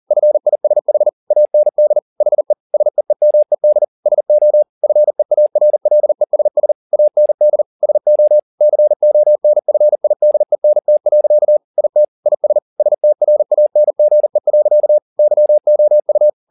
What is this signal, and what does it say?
Morse code, heard as a keyed tone, about 40 wpm, 600 Hz. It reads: FISH AND HE SEEMED SO FEARLESS AND SO CONFIDENT. IT IS STRANGE. YOU
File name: the old man and the sea 2 40wpm